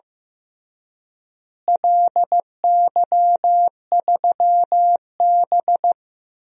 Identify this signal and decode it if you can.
Morse code, heard as a keyed tone, about 16 wpm, 700 Hz. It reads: LY3B
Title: DXCC-100 Call Signs - 4 Characters - Repeated 3x - 15wpm